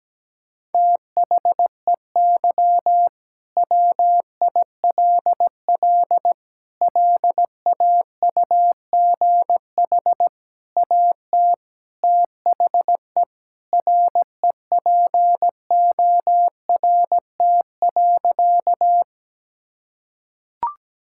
Morse code, heard as a keyed tone, about 17 wpm, 700 Hz. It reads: THEY WILL LAUGH AT THE REPORT.
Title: Sentences from Top 1000 Words - Review Part 6 - 17wpm